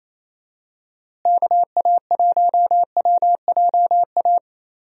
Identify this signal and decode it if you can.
Morse code, heard as a keyed tone, about 28 wpm, 700 Hz. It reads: KA1WJA
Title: USA Call Signs - 6 Characters - Repeated 3x - 28wpm